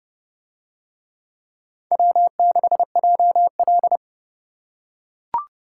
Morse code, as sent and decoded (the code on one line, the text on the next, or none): .-- -.... .--- .-..
W6JL